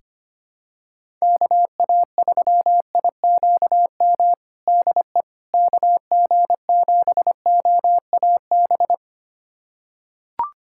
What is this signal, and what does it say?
Morse code, heard as a keyed tone, about 25 wpm, 700 Hz. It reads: KA3IQM DE KG7OAB
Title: USA Callsign DE Callsign - 6 Characters - 25wpm